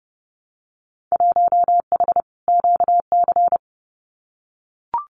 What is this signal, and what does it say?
Morse code, heard as a keyed tone, about 31 wpm, 700 Hz. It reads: 1H QC
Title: Winter Field Day Exchanges 30wpm